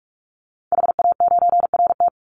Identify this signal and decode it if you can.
Morse code, heard as a keyed tone, about 44 wpm, 700 Hz. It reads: HA9RT
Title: Call Signs - 45wpm